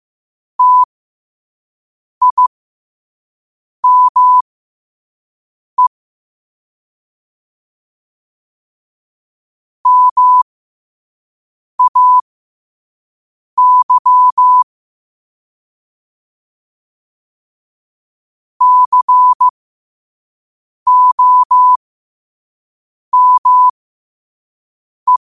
Morse code, heard as a keyed tone, about 15 wpm, spaced as for 5 wpm, 1000 Hz. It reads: TIME MAY COME
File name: qotdfiles / 5wpm